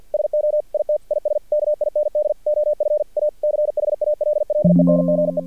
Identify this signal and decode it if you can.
Morse code, heard as a keyed tone, about 50 wpm, 600 Hz. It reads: SO IT IS KIND OF A CHALLENGE
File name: archived / CRSnet